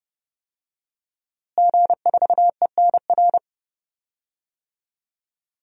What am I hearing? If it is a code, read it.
Morse code, heard as a keyed tone, about 30 wpm, 700 Hz. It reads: G4ENR
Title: DXCC-100 Call Signs - 5 Characters - Repeated 3x - 30wpm